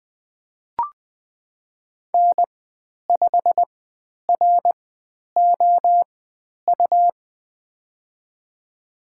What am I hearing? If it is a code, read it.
Morse code, heard as a keyed tone, about 20 wpm, spaced as for 10 wpm, 700 Hz. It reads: N5ROU
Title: 102 Lesson 17g - Letter u - Call Signs 20wpm